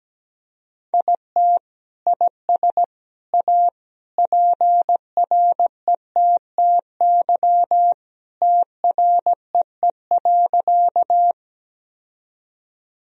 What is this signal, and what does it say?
Morse code, heard as a keyed tone, about 17 wpm, 700 Hz. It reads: IT IS A PRETTY TREE.
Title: Sentences from Top 1000 Words - Review Part 1 - 17wpm